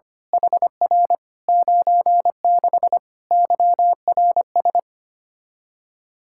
Morse code, heard as a keyed tone, about 25 wpm, 700 Hz. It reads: HR 96 YRS
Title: Practice QSO - Exchange 3 - More Info - 25wpm